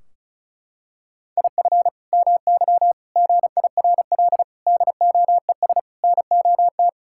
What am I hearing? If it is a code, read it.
Morse code, heard as a keyed tone, about 34 wpm, 700 Hz. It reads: IF MY GIRL DOES NOT
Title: Sentences from Top 1600 Words 35wpm